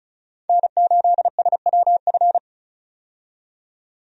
Morse code, heard as a keyed tone, about 34 wpm, 700 Hz. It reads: N8SWF
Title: USA Call Signs - 5 Characters - 35wpm